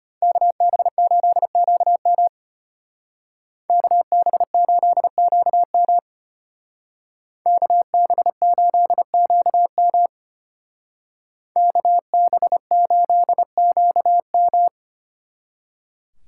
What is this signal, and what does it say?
Morse code, heard as a keyed tone, about 33 wpm, 700 Hz. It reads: KB8QM KB8QM KB8QM KB8QM